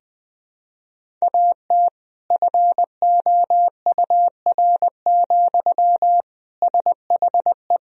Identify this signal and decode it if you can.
Morse code, heard as a keyed tone, about 20 wpm, 700 Hz. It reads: AT FOUR, SHE
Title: Sentences from Top 300 Words and Abbreviations 20wpm